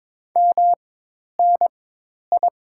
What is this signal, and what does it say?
Morse code, heard as a keyed tone, about 21 wpm, spaced as for 10 wpm, 690 Hz. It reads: MNI
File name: Top 90 QSO Elements - 4x Character Spacing - 22wpm